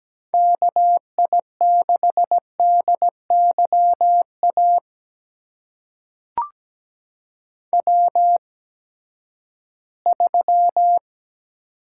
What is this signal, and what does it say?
Morse code, heard as a keyed tone, about 17 wpm, spaced as for 12 wpm, 700 Hz. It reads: KI6DYA W3
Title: USA Call Signs - 8x Character Spacing - 17wpm